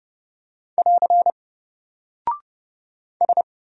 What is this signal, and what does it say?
Morse code, heard as a keyed tone, about 30 wpm, 700 Hz. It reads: <AR> S
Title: Single Character (Letter:Number:Punctuation:Prosign) Alt 30wpm